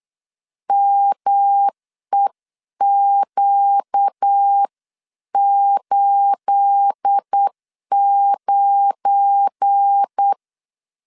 Morse code, heard as a keyed tone, about 8 wpm, 800 Hz. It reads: MEQ89
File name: CW / DET / 1 9 1500C S B 8wpm 5624